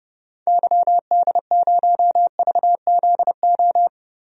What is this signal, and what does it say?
Morse code, heard as a keyed tone, about 30 wpm, 700 Hz. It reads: YD0VZO
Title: DXCC-100 Call Signs - 6 Characters - 30wpm